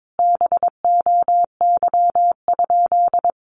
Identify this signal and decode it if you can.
Morse code, heard as a keyed tone, about 22 wpm, 700 Hz. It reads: BOY?
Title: Sentences from Top 1000 Words 22wpm